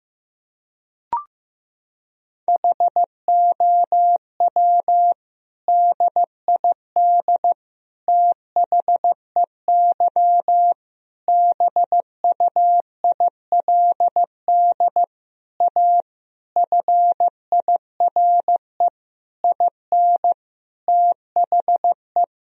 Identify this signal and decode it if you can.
Morse code, heard as a keyed tone, about 15 wpm, 700 Hz. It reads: HOW DID THEY BUILD A FIRE IN THE
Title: Sentences from Top 1000 Words - Review Part 3 - 15wpm